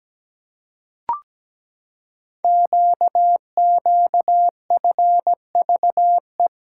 Morse code, heard as a keyed tone, about 17 wpm, 690 Hz. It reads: QQFVE
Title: Code Groups - 5 Letter:Numbers - 17wpm